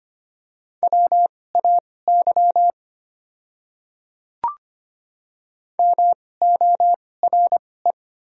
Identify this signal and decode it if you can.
Morse code, heard as a keyed tone, about 26 wpm, spaced as for 18 wpm, 690 Hz. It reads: WAY MORE
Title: 2x Character Spacing - Top 100 Words - 25wpm